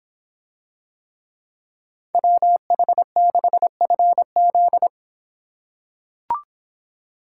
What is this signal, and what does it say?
Morse code, heard as a keyed tone, about 26 wpm, 700 Hz. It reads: WH6FZ